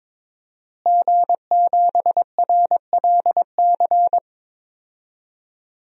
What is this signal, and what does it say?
Morse code, heard as a keyed tone, about 22 wpm, 700 Hz. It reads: G7RLC
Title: DXCC-100 Call Signs - 5 Characters - Repeated 2x - 22wpm